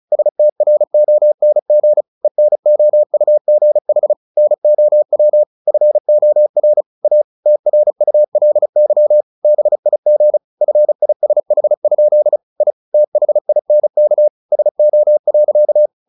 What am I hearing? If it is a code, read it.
Morse code, heard as a keyed tone, about 34 wpm, 600 Hz. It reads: STRONG ENOUGH NOW FOR A TRULY BIG FISH? I THINK SO.
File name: the old man and the sea / the old man and the sea 1 35wpm